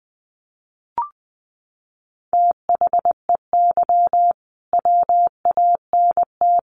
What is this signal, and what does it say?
Morse code, heard as a keyed tone, about 20 wpm, 700 Hz. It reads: THEY WANT
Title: Sentences from Top 200 Words 20wpm